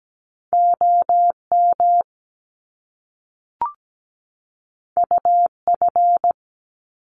Morse code, heard as a keyed tone, about 17 wpm, 700 Hz. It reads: OM UF
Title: Top Two Letter Combinations in Words 17wpm